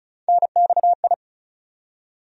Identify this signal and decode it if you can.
Morse code, heard as a keyed tone, about 35 wpm, 700 Hz. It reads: NXI
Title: USA Call Sign Postfixes - 3 Letters 35wpm